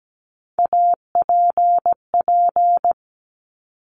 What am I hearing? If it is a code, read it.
Morse code, heard as a keyed tone, about 17 wpm, 690 Hz. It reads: APP
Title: Top Three Letter Combinations in Words 17wpm